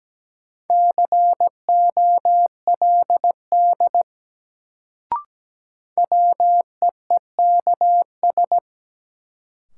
Morse code, heard as a keyed tone, about 17 wpm, 700 Hz. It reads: COLD WEEKS
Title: Sets of 5 Words - New Word Drill - 17wpm